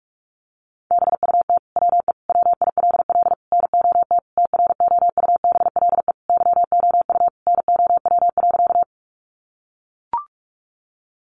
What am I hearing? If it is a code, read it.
Morse code, heard as a keyed tone, about 44 wpm, 700 Hz. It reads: BUT WE WILL NOT TROUBLE YOU NOW.
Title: Sentences from Top 1200 Words 45wpm